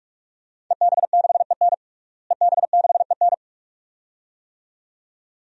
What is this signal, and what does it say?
Morse code, heard as a keyed tone, about 46 wpm, 700 Hz. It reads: EB6EN EB6EN
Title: DXCC-100 Call Signs - 5 Characters - Repeated 2x - 45wpm